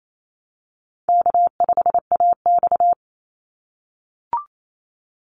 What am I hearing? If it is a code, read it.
Morse code, heard as a keyed tone, about 28 wpm, 700 Hz. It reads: K5AX
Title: Easy Call Signs - 28wpm